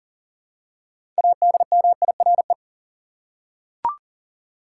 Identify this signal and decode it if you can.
Morse code, heard as a keyed tone, about 40 wpm, 700 Hz. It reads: ADMIRE
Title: Six Letter Words - 40wpm